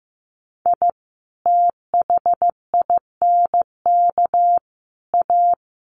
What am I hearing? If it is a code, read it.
Morse code, heard as a keyed tone, about 15 wpm, 700 Hz. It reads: I THINK A
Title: Sets of 5 Words 15wpm